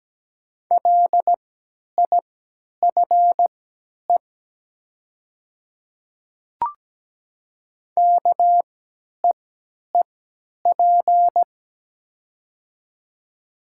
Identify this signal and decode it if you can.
Morse code, heard as a keyed tone, about 17 wpm, spaced as for 10 wpm, 700 Hz. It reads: LIFE KEEP
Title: 3x Character Spacing - Top 300 Words - 17wpm